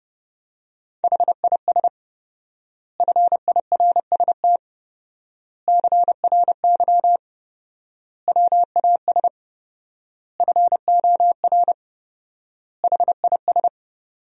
Morse code, heard as a keyed tone, about 30 wpm, 700 Hz. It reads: HIS FIRST CRY WAS FOR HIS
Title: 4x Word-Spacing - Sentences from Top 500 Words - 30wpm